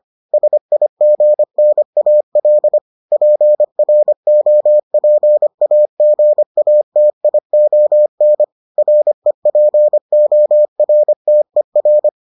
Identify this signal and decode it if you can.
Morse code, heard as a keyed tone, about 25 wpm, 600 Hz. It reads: SIGNAL PROPAGATION REPORTER